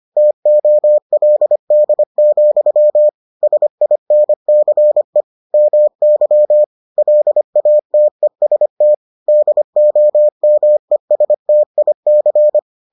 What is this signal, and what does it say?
Morse code, heard as a keyed tone, about 25 wpm, 600 Hz. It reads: TOLD, SINCE MY LATEST DOMESTIC